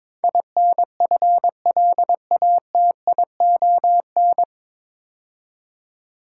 Morse code, heard as a keyed tone, about 22 wpm, 700 Hz. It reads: INFLATION
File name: 9 Letter Words - First 3 Letters Given - 22wpm